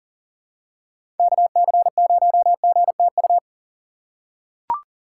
Morse code, heard as a keyed tone, about 40 wpm, 700 Hz. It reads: KC0GTU
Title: USA Call Signs - 6 Characters - 4x Character Spacing - 40wpm